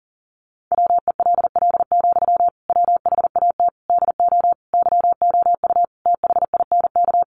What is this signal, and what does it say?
Morse code, heard as a keyed tone, about 40 wpm, 700 Hz. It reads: WELL, WHAT DO YOU THINK
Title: Sentences from Top 200 Words 40wpm